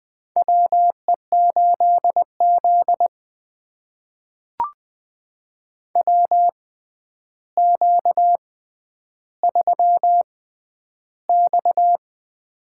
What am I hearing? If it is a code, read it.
Morse code, heard as a keyed tone, about 20 wpm, spaced as for 10 wpm, 700 Hz. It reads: WE8Z WQ3X
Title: USA Call Signs - 4 Characters - 6x Character Spacing - 20wpm